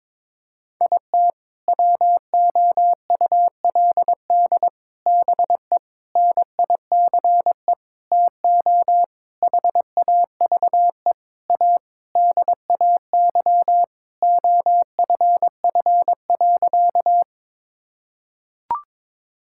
Morse code, heard as a keyed tone, about 22 wpm, 700 Hz. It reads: IT WOULD BE NICE TO HAVE A DAY OFF.
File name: Sentences with 7-Word Phrases - 22wpm